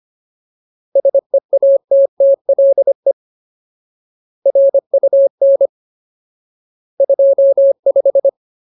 Morse code, heard as a keyed tone, about 25 wpm, 550 Hz. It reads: SEATTLE RUN 25
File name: Ultimate Word List / 25WPM Ultimate